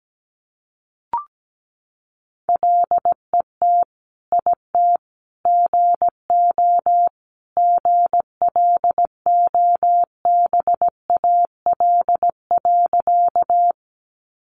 Morse code, heard as a keyed tone, about 17 wpm, 700 Hz. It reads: LET IT GO GLOBAL.
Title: Sentences from Top 1200 Words 17wpm